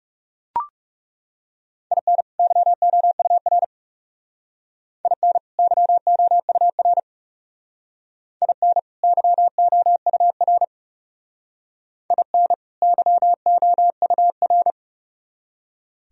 Morse code, heard as a keyed tone, about 37 wpm, 700 Hz. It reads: IN YOUR IN YOUR IN YOUR IN YOUR
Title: Speed-Racing - Sets of 2 Words 30wpm